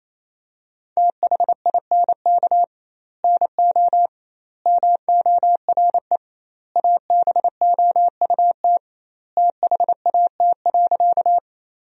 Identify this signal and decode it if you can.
Morse code, heard as a keyed tone, about 28 wpm, 700 Hz. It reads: THINK NO MORE ABOUT THAT.
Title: 2x Word-Spacing - Sentences from Top 200 Words - 28wpm